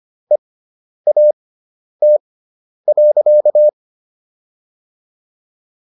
Morse code, heard as a keyed tone, about 25 wpm, spaced as for 10 wpm, 600 Hz. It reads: EAT.